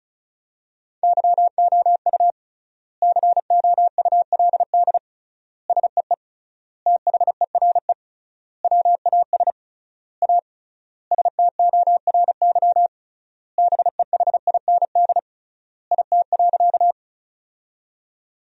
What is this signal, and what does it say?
Morse code, heard as a keyed tone, about 34 wpm, 700 Hz. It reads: YOU COULD SEE THERE WAS A STORY BEHIND IT.
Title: 3x Word-Spacing - Sentences from Top 500 Words - 35wpm